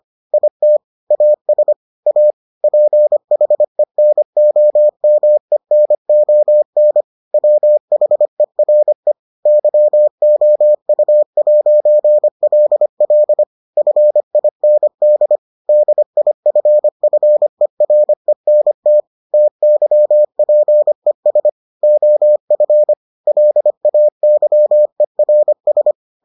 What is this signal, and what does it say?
Morse code, heard as a keyed tone, about 25 wpm, 600 Hz. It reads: IT AS A PHENOMENON WHERE YOU'LL FIND DIFFERENT TYPES OF LAYERS